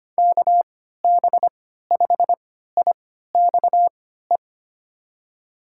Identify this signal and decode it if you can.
Morse code, heard as a keyed tone, about 25 wpm, spaced as for 14 wpm, 700 Hz. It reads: KB5IXE
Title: USA Call Signs - 6 Characters - 3x Character Spacing - 25wpm